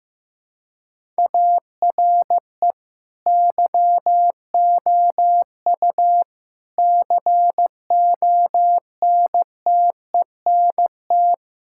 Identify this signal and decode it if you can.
Morse code, heard as a keyed tone, about 15 wpm, 700 Hz. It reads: ARE YOU CONTENT
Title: Sentences from Top 1000 Words - Review Part 6 - 15wpm